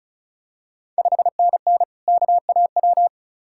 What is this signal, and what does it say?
Morse code, heard as a keyed tone, about 36 wpm, 700 Hz. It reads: 5NN KAW
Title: Ontario QSO Party Exchange 35wpm